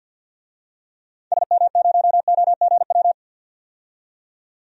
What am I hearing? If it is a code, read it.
Morse code, heard as a keyed tone, about 46 wpm, 700 Hz. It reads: SM0OGW
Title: DXCC-100 Call Signs - 6 Characters - 50wpm